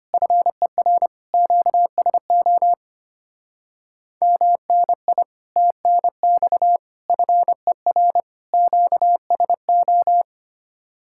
Morse code, heard as a keyed tone, about 31 wpm, 700 Hz. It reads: FER QSO MNI TNX FER QSO
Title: Speed-Racing Common QSO Phrases - 25wpm